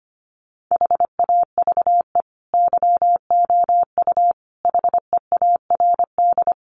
Morse code, heard as a keyed tone, about 25 wpm, 700 Hz. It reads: HAVE YOU HEARD
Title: Sentences from Top 1000 Words 25wpm